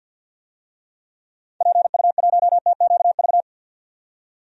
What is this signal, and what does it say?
Morse code, heard as a keyed tone, about 50 wpm, 700 Hz. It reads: PU1TQV
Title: DXCC-100 Call Signs - 6 Characters - 50wpm